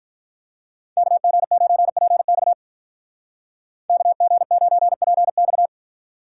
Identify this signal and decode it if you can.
Morse code, heard as a keyed tone, about 45 wpm, 700 Hz. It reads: KG9PX KG9PX